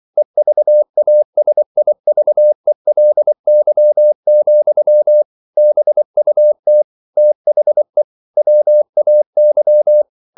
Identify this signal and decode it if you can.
Morse code, heard as a keyed tone, about 24 wpm, 600 Hz. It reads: EVASIVELY, BUT THE WAY